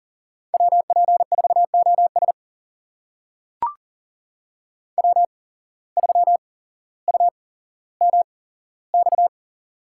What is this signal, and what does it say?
Morse code, heard as a keyed tone, about 40 wpm, spaced as for 17 wpm, 700 Hz. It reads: WP4OS W3UMX
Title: USA Call Signs - 5 Characters - 8x Character Spacing - 40wpm